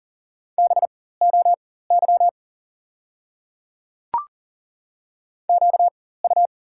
Morse code, heard as a keyed tone, about 40 wpm, spaced as for 19 wpm, 690 Hz. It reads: BOY QU
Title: 4x Character Spacing - Top 500 Words - 40wpm